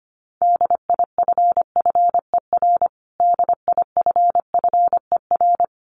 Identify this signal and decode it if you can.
Morse code, heard as a keyed tone, about 25 wpm, 700 Hz. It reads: DIFFER DIFFER
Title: Top 500 Words Repeated 25wpm